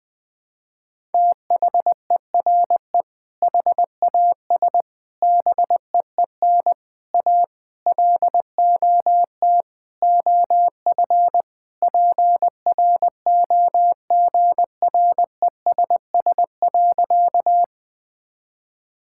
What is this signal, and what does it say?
Morse code, heard as a keyed tone, about 20 wpm, 700 Hz. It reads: THERE HAS BEEN A LOT OF PROGRESS.
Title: Sentences with 6-Word Phrases - Spoken Beginning - 20wpm